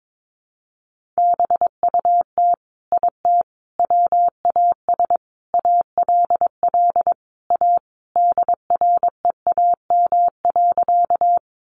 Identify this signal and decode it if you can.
Morse code, heard as a keyed tone, about 22 wpm, 700 Hz. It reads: BUT IT WAS ALL A DREAM.
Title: Sentences from Top 1500 Words 22wpm